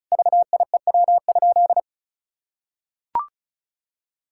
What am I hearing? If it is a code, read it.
Morse code, heard as a keyed tone, about 36 wpm, 700 Hz. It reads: VIEW?
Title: Sentences from Top 1000 Words - Review Part 6 - 35wpm